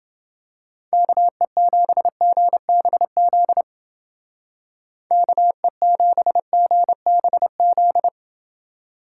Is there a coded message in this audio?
Yes: Morse code from 0.9 to 8.1 s, heard as a keyed tone, about 28 wpm, 700 Hz. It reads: KE7GBZ KE7GBZ